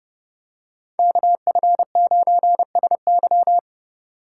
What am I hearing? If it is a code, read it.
Morse code, heard as a keyed tone, about 29 wpm, 700 Hz. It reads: KF9SY